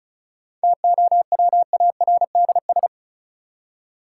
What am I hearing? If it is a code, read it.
Morse code, heard as a keyed tone, about 36 wpm, 700 Hz. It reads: TOWARDS
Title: Seven Letter Words - 35wpm